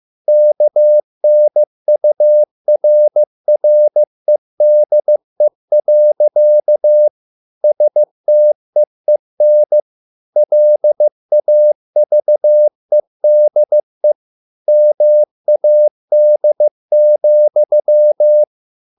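Morse code, heard as a keyed tone, about 15 wpm, 600 Hz. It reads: KNURREDE. STEEN LAVEDE MAD,
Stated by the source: Morse code at a slow 15 wpm from CW med Gnister.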